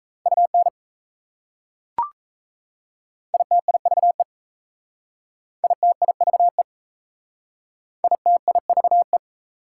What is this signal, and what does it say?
Morse code, heard as a keyed tone, about 37 wpm, 690 Hz. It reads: UN ITIVE ITIVE ITIVE